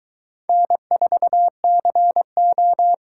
Morse code, heard as a keyed tone, about 22 wpm, 700 Hz. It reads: N4CO